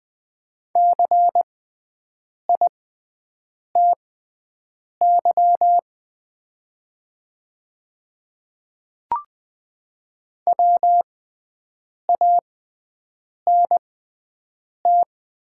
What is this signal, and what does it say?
Morse code, heard as a keyed tone, about 20 wpm, spaced as for 7 wpm, 700 Hz. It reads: CITY WANT
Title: 6x Character Spacing - Top 300 Words - 20wpm